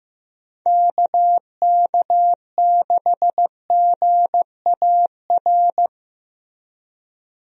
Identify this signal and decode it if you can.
Morse code, heard as a keyed tone, about 15 wpm, 700 Hz. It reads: KK6GAR
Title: USA Call Signs - 6 Characters - Repeated 2x - 15wpm